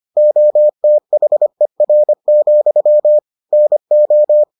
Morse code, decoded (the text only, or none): OTHER, NO